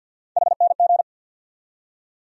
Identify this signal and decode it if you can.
Morse code, heard as a keyed tone, about 47 wpm, 700 Hz. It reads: HNG
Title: USA Call Sign Postfixes - 3 Letters 50wpm